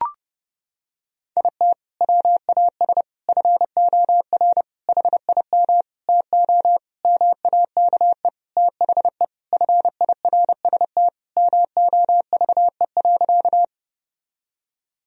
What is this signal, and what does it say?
Morse code, heard as a keyed tone, about 32 wpm, 700 Hz. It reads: IT WAS FOR HIM TO MAKE THE FIRST MOVE.
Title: Sentences from Top 1000 Words - Review Part 1 - 30wpm